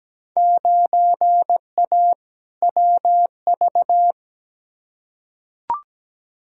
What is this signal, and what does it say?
Morse code, heard as a keyed tone, about 17 wpm, 700 Hz. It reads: 9A WV
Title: Field Day Exchanges - 17wpm